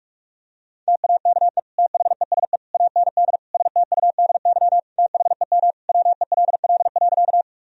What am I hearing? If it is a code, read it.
Morse code, heard as a keyed tone, about 46 wpm, 700 Hz. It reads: TAKE THESE AND STUDY THEM WELL.
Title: Sentences from Top 1000 Words - Review Part 1 - 45wpm